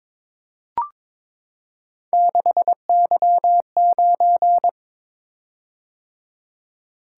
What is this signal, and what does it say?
Morse code, heard as a keyed tone, about 22 wpm, 700 Hz. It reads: 6Y9
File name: practice / 22wpm / Three Letter-Number Combinations - 22wpm